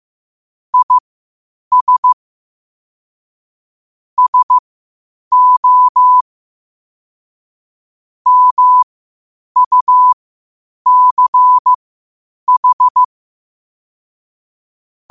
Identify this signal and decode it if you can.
Morse code, heard as a keyed tone, about 15 wpm, spaced as for 8 wpm, 1000 Hz. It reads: IS SO MUCH
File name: qotdfiles / 8wpm